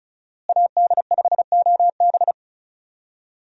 Morse code, heard as a keyed tone, about 36 wpm, 700 Hz. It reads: AD5OB
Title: USA Call Signs - 5 Characters - Repeated 2x - 35wpm